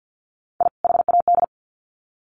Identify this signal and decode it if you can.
Morse code, heard as a keyed tone, about 50 wpm, 700 Hz. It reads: I HAD
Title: Sets of 2 Words 50wpm